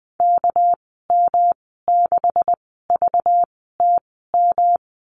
Morse code, heard as a keyed tone, about 20 wpm, spaced as for 14 wpm, 700 Hz. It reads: KM6VTM
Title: 238 Lesson 38g - Letter q - Call Signs 20wpm